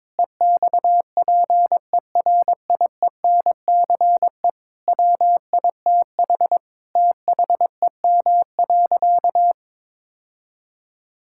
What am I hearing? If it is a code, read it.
Morse code, heard as a keyed tone, about 22 wpm, 700 Hz. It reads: EXPERIENCE WITH THEM.
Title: Sentences from Top 1000 Words - Review Part 6 - 22wpm